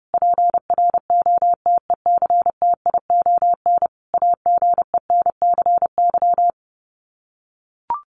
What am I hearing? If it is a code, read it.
Morse code, heard as a keyed tone, about 30 wpm, 700 Hz. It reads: PROTECTION AGENCY
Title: Sets of 5 Words Encore2 30wpm